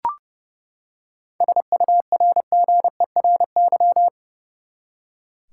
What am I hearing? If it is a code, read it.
Morse code, heard as a keyed tone, about 30 wpm, 700 Hz. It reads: SURGERY
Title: Seven Letter Words - 30wpm